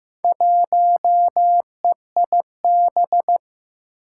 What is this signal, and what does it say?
Morse code, heard as a keyed tone, about 15 wpm, 700 Hz. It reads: 1EIB
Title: Four Challenging Letter-Number Combinations 15wpm